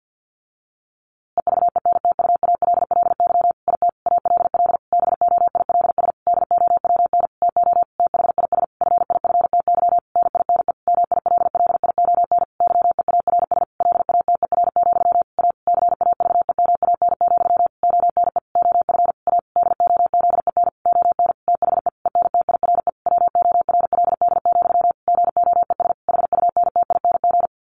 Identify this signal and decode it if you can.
Morse code, heard as a keyed tone, about 48 wpm, 700 Hz. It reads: EVENTUALLY IT ALL BOILS DOWN TO THIS FIFTY NINE MILLION YEARS LATER, A CAVEMAN, ONE OF A DOZEN ON THE ENTIRE WORLD, GOES HUNTING
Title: A Sound Of Thunder 50wpm